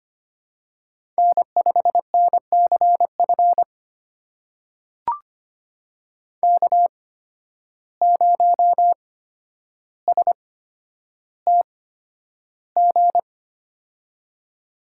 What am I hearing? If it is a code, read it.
Morse code, heard as a keyed tone, about 25 wpm, spaced as for 11 wpm, 700 Hz. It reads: N5NCF K0STG
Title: USA Call Signs - 8x Character Spacing - 25wpm